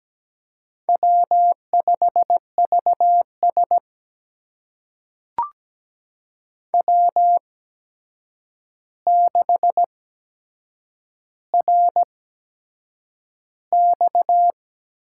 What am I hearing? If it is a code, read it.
Morse code, heard as a keyed tone, about 17 wpm, spaced as for 7 wpm, 700 Hz. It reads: W5VS W6RX
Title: USA Call Signs - 4 Characters - 8x Character Spacing - 17wpm